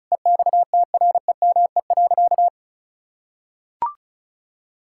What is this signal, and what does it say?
Morse code, heard as a keyed tone, about 36 wpm, 700 Hz. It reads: EXTREME.
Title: Sentences from Top 1600 Words 35wpm